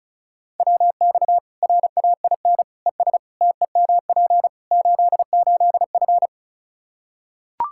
Morse code, heard as a keyed tone, about 36 wpm, 700 Hz. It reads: WX RAIN ES TEMP 88F
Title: Practice QSO - Exchange 2 - Info - 35wpm